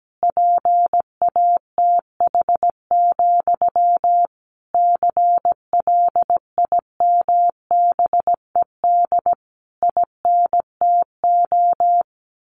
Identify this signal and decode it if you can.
Morse code, heard as a keyed tone, about 17 wpm, 700 Hz. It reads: PATH, CLIMBED INTO
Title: A Sound Of Thunder 17wpm